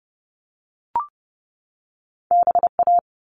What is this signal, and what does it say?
Morse code, heard as a keyed tone, about 30 wpm, 700 Hz. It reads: BA
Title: All Cut Numbers - 2-Digit 30wpm